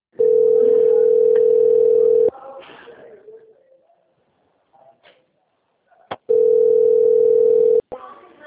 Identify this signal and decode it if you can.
Ringback tone (North America), 440 + 480 Hz.